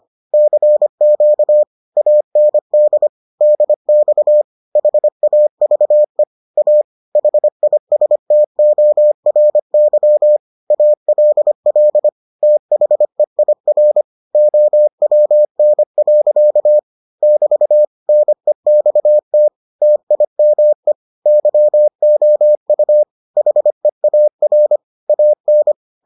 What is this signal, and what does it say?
Morse code, heard as a keyed tone, about 25 wpm, 600 Hz. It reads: CQ AND DX HAVE A HISTORY ALL THEIR OWN. <BT> NEXT TIME YOU HEAR AN